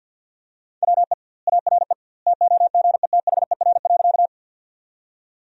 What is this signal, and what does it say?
Morse code, heard as a keyed tone, about 53 wpm, 700 Hz. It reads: WE ARE TOGETHER.
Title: 2x Word-Spacing - Sentences from Top 300 Words - 50wpm